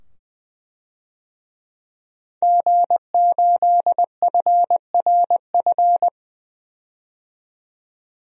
Morse code, heard as a keyed tone, about 20 wpm, 700 Hz. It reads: G8FRF
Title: DXCC-100 Call Signs - 5 Characters - 20wpm